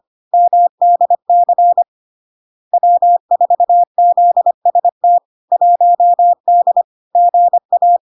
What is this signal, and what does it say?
Morse code, heard as a keyed tone, about 25 wpm, 700 Hz. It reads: MDC W4ZST 1D GA
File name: Log-in-CW-25wpm-700-Hz